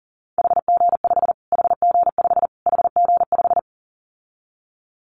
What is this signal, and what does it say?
Morse code, heard as a keyed tone, about 40 wpm, 700 Hz. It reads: HG5 HG5 HG5
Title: Callsign Prefixes for Top 25 DXCC - Repeated 3x - 40wpm